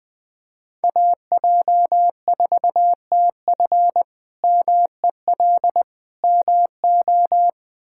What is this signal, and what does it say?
Morse code, heard as a keyed tone, about 21 wpm, 700 Hz. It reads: AJ4TF MEL MO
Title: K1USN SST QSO Practice - 20wpm